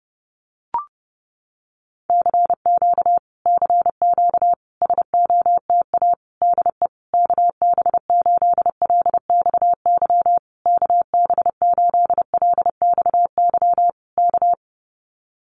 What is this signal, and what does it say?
Morse code, heard as a keyed tone, about 30 wpm, 700 Hz. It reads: CQ CQ SOTA DE KB8LXY KB8LXY K
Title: Common QSO Phrases Encore 30wpm